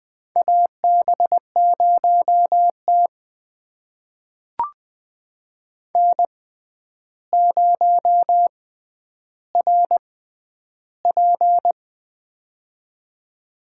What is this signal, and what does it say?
Morse code, heard as a keyed tone, about 20 wpm, spaced as for 10 wpm, 700 Hz. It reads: AB0T N0RP
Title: USA Call Signs - 4 Characters - 6x Character Spacing - 20wpm